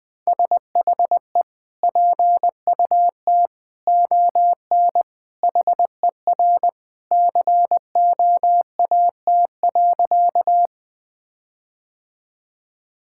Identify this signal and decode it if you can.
Morse code, heard as a keyed tone, about 20 wpm, 700 Hz. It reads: SHE PUT ON HER COAT.
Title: Sentences from Top 1600 Words 20wpm